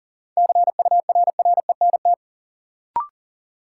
Morse code, heard as a keyed tone, about 40 wpm, 700 Hz. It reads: CURRENT
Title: Sentences from Top 1600 Words - New Word Drill Alt - 40wpm